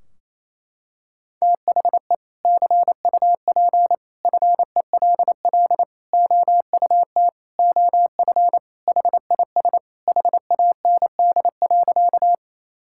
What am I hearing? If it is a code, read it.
Morse code, heard as a keyed tone, about 28 wpm, 700 Hz. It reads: THE CUP FELL OUT OF HIS HAND.
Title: Sentences from Top 1600 Words 28wpm